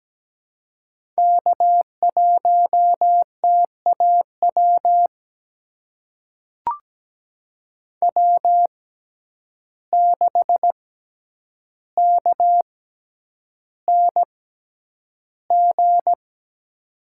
Morse code, heard as a keyed tone, about 17 wpm, spaced as for 9 wpm, 700 Hz. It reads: K1TAW W6KNG
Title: USA Call Signs - 5 Characters - 6x Character Spacing - 17wpm